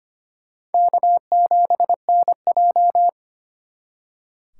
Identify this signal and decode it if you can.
Morse code, heard as a keyed tone, about 32 wpm, 700 Hz. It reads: K7NJ